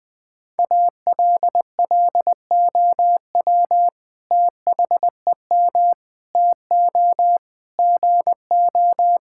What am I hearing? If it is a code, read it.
Morse code, heard as a keyed tone, about 20 wpm, 700 Hz. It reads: ALLOW THEM TO GO
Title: Sentences from Top 1000 Words - Review Part 5 - 20wpm